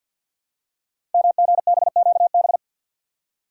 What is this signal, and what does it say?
Morse code, heard as a keyed tone, about 46 wpm, 700 Hz. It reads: MGBQB
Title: Code Groups - 5 Letters - 50wpm